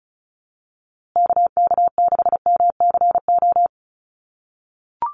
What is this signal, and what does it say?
Morse code, heard as a keyed tone, about 34 wpm, 700 Hz. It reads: KK6MCO
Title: USA Call Signs Repeated 3x 35wpm